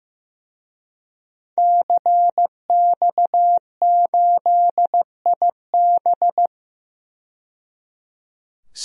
Morse code, heard as a keyed tone, about 15 wpm, 700 Hz. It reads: CX8IB
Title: DXCC-100 Call Signs - 5 Characters - 15wpm